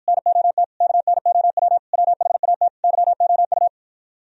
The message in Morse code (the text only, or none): NOT KNOW WHAT YOU